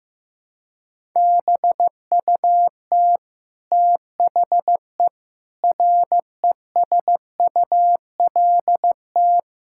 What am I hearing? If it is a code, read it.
Morse code, heard as a keyed tone, about 15 wpm, 700 Hz. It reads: BUT THE RESULT
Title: Sentences from Top 1000 Words - Review Part 5 - 15wpm